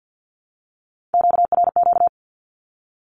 Morse code, heard as a keyed tone, about 50 wpm, 700 Hz. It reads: QRQ
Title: Q-Codes 50wpm